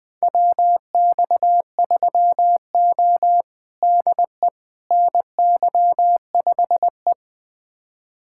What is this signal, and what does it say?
Morse code, heard as a keyed tone, about 20 wpm, 700 Hz. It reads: WX3O DE NY5E
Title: USA Callsign DE Callsign - 4 Characters - 20wpm